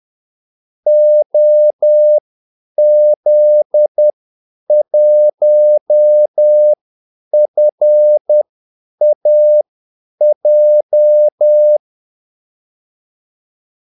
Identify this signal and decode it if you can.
Morse code, heard as a keyed tone, about 10 wpm, spaced as for 8 wpm, 600 Hz. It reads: OZ1FAJ
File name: Callsigns-10-8wpm